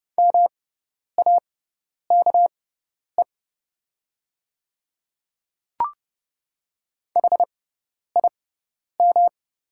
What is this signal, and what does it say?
Morse code, heard as a keyed tone, about 30 wpm, spaced as for 10 wpm, 700 Hz. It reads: MAKE HIM
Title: 6x Character Spacing - Top 100 Words - 30wpm